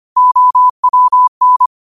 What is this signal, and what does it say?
Morse code, heard as a keyed tone, about 25 wpm, 1000 Hz. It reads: OWN
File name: qotdfiles / 25wpm